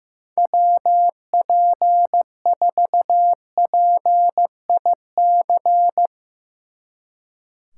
Morse code, heard as a keyed tone, about 15 wpm, 700 Hz. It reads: WP4PIC
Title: Speed Racing - USA 2x3 Call Signs 15wpm